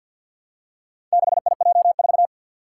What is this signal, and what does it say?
Morse code, heard as a keyed tone, about 51 wpm, 700 Hz. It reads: 6IJ4
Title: Four Challenging Letter-Number Combinations 50wpm